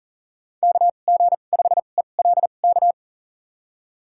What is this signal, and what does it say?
Morse code, heard as a keyed tone, about 39 wpm, spaced as for 29 wpm, 700 Hz. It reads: KG5ELK
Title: USA Call Signs - 6 Characters - 2x Character Spacing - 40wpm